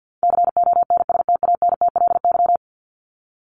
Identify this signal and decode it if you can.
Morse code, heard as a keyed tone, about 50 wpm, 700 Hz. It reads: CONSTANTLY
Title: LY Suffix - Top 100 Words with Base Words 50wpm